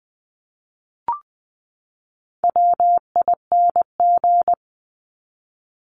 Morse code, heard as a keyed tone, about 20 wpm, 700 Hz. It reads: WING
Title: Top 1001-1250 Words 20wpm